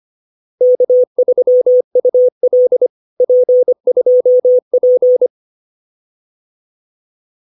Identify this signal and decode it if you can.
Morse code, heard as a keyed tone, about 25 wpm, 500 Hz. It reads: K3UL P2P